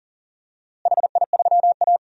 Morse code, heard as a keyed tone, about 39 wpm, 700 Hz. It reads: HI3A
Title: DXCC-100 Call Signs - 4 Characters - Repeated 2x - 40wpm